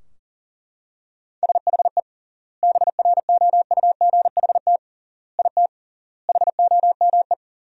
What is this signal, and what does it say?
Morse code, heard as a keyed tone, about 37 wpm, 700 Hz. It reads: SHE BROUGHT IT HOME
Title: 3x Word-Spacing - Sentences from Top 500 Words - 40wpm